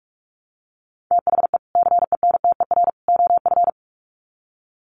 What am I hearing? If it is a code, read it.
Morse code, heard as a keyed tone, about 44 wpm, 700 Hz. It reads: THE CENTER OF
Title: Sets of 3 Words 45wpm